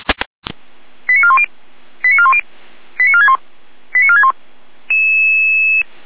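Five-tone selective call (ZVEI), heard as quick runs of five tones, tones of about 70 ms. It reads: 89410 89410 89461 89461
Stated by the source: ZVEI five-tone selective call received by SDR.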